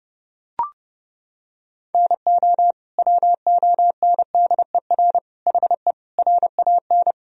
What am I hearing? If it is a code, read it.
Morse code, heard as a keyed tone, about 30 wpm, 700 Hz. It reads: NO WONDER HE RAN
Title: Sentences from Top 1000 Words - Review Part 3 - 30wpm